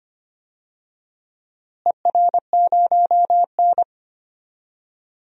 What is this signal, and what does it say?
Morse code, heard as a keyed tone, about 25 wpm, 700 Hz. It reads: ER0N
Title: DXCC-100 Call Signs - 4 Characters - Repeated 2x - 25wpm